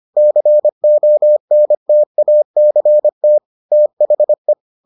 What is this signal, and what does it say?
Morse code, heard as a keyed tone, about 25 wpm, 600 Hz. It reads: CONTACT THE